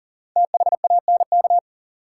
Morse code, heard as a keyed tone, about 40 wpm, 700 Hz. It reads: THANK
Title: Sets of 2 Words - New Word Drill - 40wpm